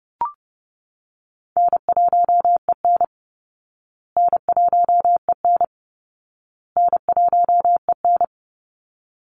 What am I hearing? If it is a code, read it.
Morse code, heard as a keyed tone, about 30 wpm, 700 Hz. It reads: N1EN N1EN N1EN
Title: Call Signs Repeated 2x - 30wpm